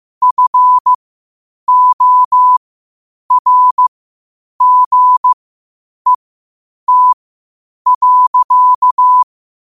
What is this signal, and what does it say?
Morse code, heard as a keyed tone, about 15 wpm, spaced as for 8 wpm, 1000 Hz. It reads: FORGET.
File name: qotdfiles / 8wpm